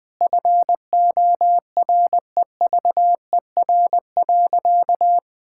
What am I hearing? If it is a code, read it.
Morse code, heard as a keyed tone, about 20 wpm, 700 Hz. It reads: FOREVER.
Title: Sentences from Top 1600 Words 20wpm